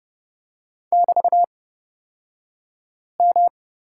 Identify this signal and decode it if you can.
Morse code, heard as a keyed tone, about 29 wpm, spaced as for 5 wpm, 700 Hz. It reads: <BT>M
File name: Lesson 36d - BT - Two Characters 30wpm